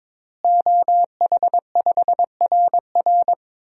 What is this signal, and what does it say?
Morse code, heard as a keyed tone, about 22 wpm, 700 Hz. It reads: OH5RR
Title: DXCC-100 Call Signs - 5 Characters - Repeated 3x - 22wpm